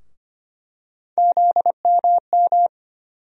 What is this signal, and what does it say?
Morse code, heard as a keyed tone, about 25 wpm, 700 Hz. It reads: ZMM
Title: USA Call Sign Postfixes - 3 Letters 25wpm